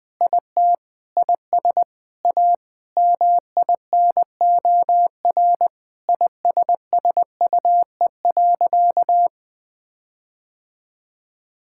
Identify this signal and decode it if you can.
Morse code, heard as a keyed tone, about 20 wpm, 700 Hz. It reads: IT IS A MINOR ISSUE.
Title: Sentences with 6-Word Phrases - 20wpm